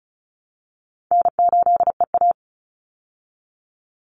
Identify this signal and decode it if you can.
Morse code, heard as a keyed tone, about 34 wpm, 700 Hz. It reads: N8EA
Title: Call Signs Repeated 2x - 35wpm